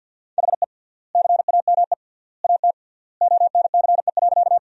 Morse code, heard as a keyed tone, about 45 wpm, 700 Hz. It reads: HE CAME AT ONCE.
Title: 3x Word-Spacing - Sentences from Top 300 Words - 50wpm